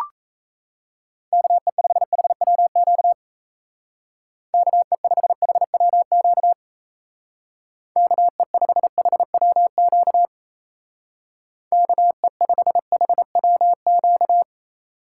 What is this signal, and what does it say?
Morse code, heard as a keyed tone, about 34 wpm, 700 Hz. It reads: KE5HWQ KE5HWQ KE5HWQ KE5HWQ
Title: Speed Racing - USA 2x3 Call Signs 28wpm